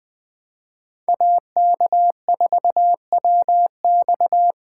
Morse code, heard as a keyed tone, about 21 wpm, 700 Hz. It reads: AK4WX